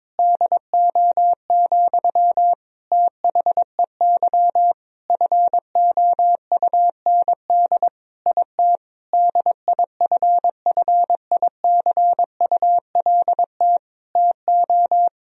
Morse code, heard as a keyed tone, about 22 wpm, 700 Hz. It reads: DO, THEY FOUND IT DIFFICULT TO
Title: Sentences from Top 1000 Words - Review Part 4 - 22wpm